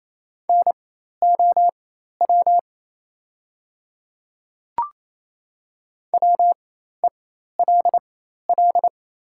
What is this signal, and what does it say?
Morse code, heard as a keyed tone, about 27 wpm, spaced as for 13 wpm, 700 Hz. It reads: NOW WELL
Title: 4x Character Spacing - Top 300 Words - 28wpm